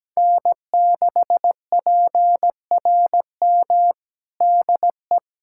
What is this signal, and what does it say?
Morse code, heard as a keyed tone, about 17 wpm, 700 Hz. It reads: N6PRM DE
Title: USA Callsign DE Callsign - 5 Characters - 17wpm